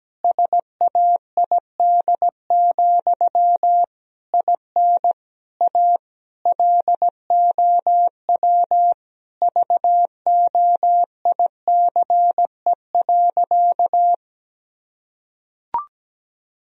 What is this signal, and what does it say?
Morse code, heard as a keyed tone, about 17 wpm, 700 Hz. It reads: SAID, IN A LOW VOICE.
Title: Sentences from Top 1000 Words - Review Part 1 - 17wpm